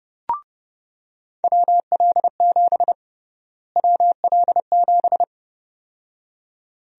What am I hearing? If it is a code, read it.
Morse code, heard as a keyed tone, about 31 wpm, 700 Hz. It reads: WL7 WL7
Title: USA Call Sign Prefixes - Repeated 2x - 30wpm